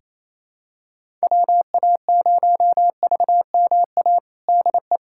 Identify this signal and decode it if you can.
Morse code, heard as a keyed tone, about 28 wpm, 700 Hz. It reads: WA0VMA DE
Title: USA Callsign DE Callsign 28wpm